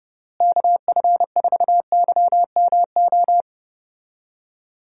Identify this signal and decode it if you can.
Morse code, heard as a keyed tone, about 31 wpm, 700 Hz. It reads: KF4YMO
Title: USA Call Signs - 6 Characters - Repeated 3x - 30wpm